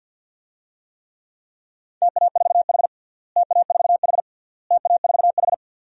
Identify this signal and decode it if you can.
Morse code, heard as a keyed tone, about 50 wpm, 700 Hz. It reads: TA4H TA4H TA4H
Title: DXCC-100 Call Signs - 4 Characters - Repeated 3x - 50wpm